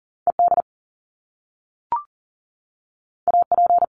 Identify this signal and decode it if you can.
Morse code, heard as a keyed tone, about 39 wpm, 700 Hz. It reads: ED AP
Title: Top Two Letter Combinations in Words 40wpm